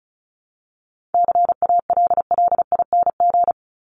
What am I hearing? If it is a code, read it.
Morse code, heard as a keyed tone, about 34 wpm, 700 Hz. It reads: CALLING
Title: ING Suffix - Top 100 Words with Base Words 35wpm